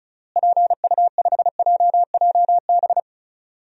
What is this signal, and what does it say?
Morse code, heard as a keyed tone, about 36 wpm, 700 Hz. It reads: PU5JJB
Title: DXCC-100 Call Signs - 6 Characters - Repeated 3x - 35wpm